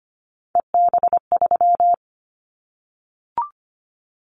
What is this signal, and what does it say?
Morse code, heard as a keyed tone, about 25 wpm, 700 Hz. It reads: EB3
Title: Callsign Prefixes for Top 25 DXCC - 25wpm